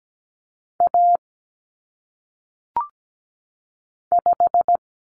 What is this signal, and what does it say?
Morse code, heard as a keyed tone, about 17 wpm, 700 Hz. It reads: A 5
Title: Common Cut Numbers - 1-Digit 17wpm